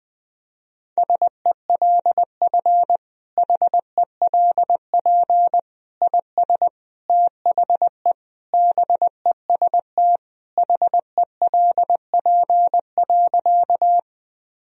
Morse code, heard as a keyed tone, about 20 wpm, 700 Hz. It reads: SELF HELP IS THE BEST HELP.
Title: Sentences from Top 1000 Words - Review Part 1 - 20wpm